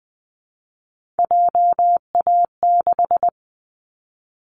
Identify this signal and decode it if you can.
Morse code, heard as a keyed tone, about 20 wpm, 700 Hz. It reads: JA6
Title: Callsign Prefixes for Top 25 DXCC - Repeated 3x - 20wpm